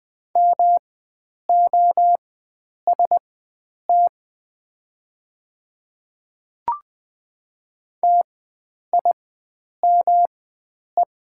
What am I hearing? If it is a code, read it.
Morse code, heard as a keyed tone, about 20 wpm, spaced as for 9 wpm, 700 Hz. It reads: MOST TIME
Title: 4x Character Spacing - Top 300 Words - 20wpm